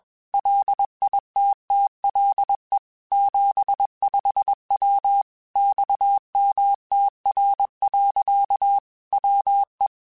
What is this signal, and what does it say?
Morse code, heard as a keyed tone, about 21 wpm, 800 Hz. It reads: LITTLE 75W XMTR. WE
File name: QSOs / QSOs-mp3s / WA7QOX 1@20